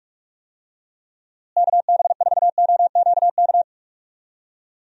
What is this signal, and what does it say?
Morse code, heard as a keyed tone, about 41 wpm, 700 Hz. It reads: KB4OQK
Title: USA Call Signs - 6 Characters - Repeated 3x - 45wpm